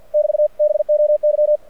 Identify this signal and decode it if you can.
Morse code, heard as a keyed tone, about 54 wpm, 600 Hz. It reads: <BT> BOY
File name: archived / CRSnet